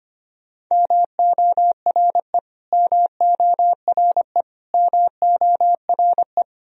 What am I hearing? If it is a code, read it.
Morse code, heard as a keyed tone, about 25 wpm, 700 Hz. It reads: MORE MORE MORE
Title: Top 100 Words Repeated 25wpm